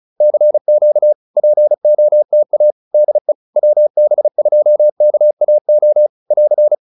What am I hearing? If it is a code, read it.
Morse code, heard as a keyed tone, about 36 wpm, 600 Hz. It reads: CQ POTA DE WB2KAO <AR>